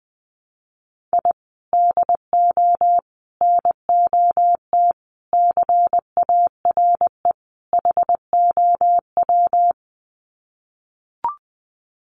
Morse code, heard as a keyed tone, about 20 wpm, 700 Hz. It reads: I DO NOT CARE HOW
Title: Sets of 5 Words Encore2 20wpm